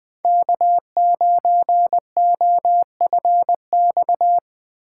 Morse code, heard as a keyed tone, about 21 wpm, 700 Hz. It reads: K9OFX